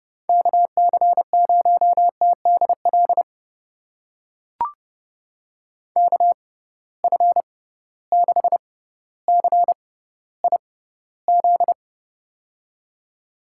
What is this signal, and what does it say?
Morse code, heard as a keyed tone, about 31 wpm, spaced as for 16 wpm, 700 Hz. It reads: KC0TDL KF6CIZ
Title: USA Call Signs - 6 Characters - 6x Character Spacing - 30wpm